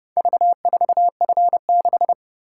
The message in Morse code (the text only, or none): V4F6